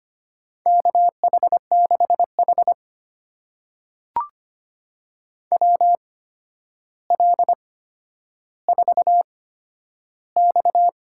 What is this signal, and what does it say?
Morse code, heard as a keyed tone, about 25 wpm, spaced as for 11 wpm, 700 Hz. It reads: KH6H WL4X
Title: USA Call Signs - 8x Character Spacing - 25wpm